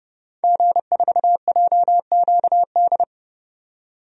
Morse code, heard as a keyed tone, about 30 wpm, 700 Hz. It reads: G4JQD
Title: DXCC-100 Call Signs - 5 Characters - 30wpm